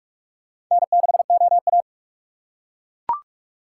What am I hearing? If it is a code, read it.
Morse code, heard as a keyed tone, about 40 wpm, 700 Hz. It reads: N6OA